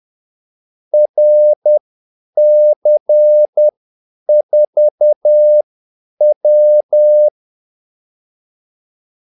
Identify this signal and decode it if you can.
Morse code, heard as a keyed tone, about 10 wpm, spaced as for 8 wpm, 600 Hz. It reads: RC4W
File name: Callsigns-10-8wpm